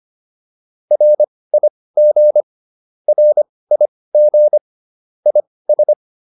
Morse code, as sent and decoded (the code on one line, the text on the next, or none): .-. .. --. / .-. .. --. / .. ...
RIG RIG IS